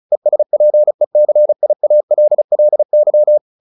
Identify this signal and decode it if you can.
Morse code, heard as a keyed tone, about 34 wpm, 600 Hz. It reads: ESPECIALLY
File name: the old man and the sea / the old man and the sea 1 35wpm